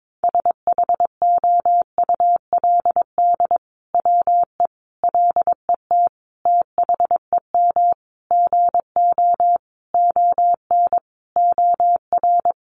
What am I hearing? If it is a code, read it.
Morse code, heard as a keyed tone, about 22 wpm, 700 Hz. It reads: SHOULD WE LET THEM GO ON OR
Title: Sentences from Top 600 Words 22wpm